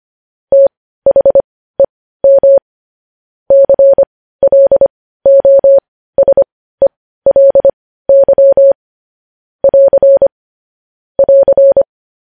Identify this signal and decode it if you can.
Morse code, heard as a keyed tone, about 25 wpm, spaced as for 15 wpm, 550 Hz. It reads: THEM CLOSELY <AR> <AR>
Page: news headlines in morse code at 15 wpm